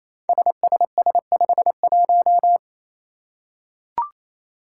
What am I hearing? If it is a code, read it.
Morse code, heard as a keyed tone, about 28 wpm, 700 Hz. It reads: SSS51
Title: Five Challenging Letter-Number Combinations 28wpm